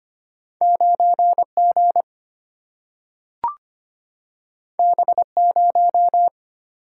Morse code, heard as a keyed tone, about 25 wpm, 690 Hz. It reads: 9G B0
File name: Two Letter-Number Combinations - 25wpm